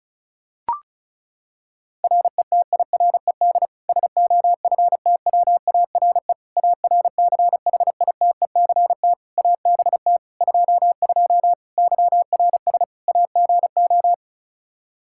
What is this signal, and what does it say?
Morse code, heard as a keyed tone, about 36 wpm, 700 Hz. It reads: RETIRED SOFTWARE ARCHITECT ABT 22 YRS AGO
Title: Practice QSO - Exchange 3 - More Info - 35wpm